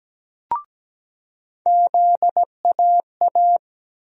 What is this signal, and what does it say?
Morse code, heard as a keyed tone, about 17 wpm, 700 Hz. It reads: ZAA
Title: USA Call Sign Postfixes - 3 Letters 17wpm